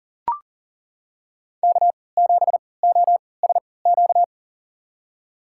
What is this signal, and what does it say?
Morse code, heard as a keyed tone, about 40 wpm, spaced as for 23 wpm, 700 Hz. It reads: K7OSQ
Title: USA Call Signs - 5 Characters - 3x Character Spacing - 40wpm